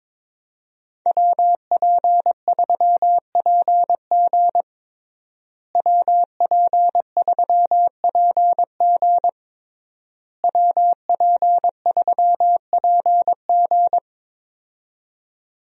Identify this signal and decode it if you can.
Morse code, heard as a keyed tone, about 22 wpm, 700 Hz. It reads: WP3PG WP3PG WP3PG
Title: USA Call Signs - 5 Characters - Repeated 3x - 22wpm